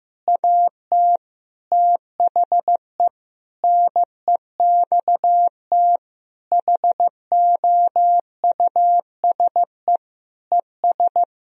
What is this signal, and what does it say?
Morse code, heard as a keyed tone, about 14 wpm, 700 Hz. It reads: AT THE NEXT HOUSE ES
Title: Sentences from Top 400 Words and Abbreviations 15wpm